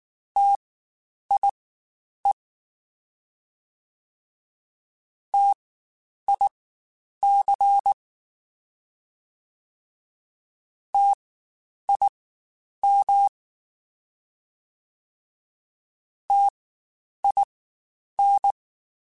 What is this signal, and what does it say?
Morse code, heard as a keyed tone, about 19 wpm, spaced as for 9 wpm, 800 Hz. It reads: TIE TIC TIM TIN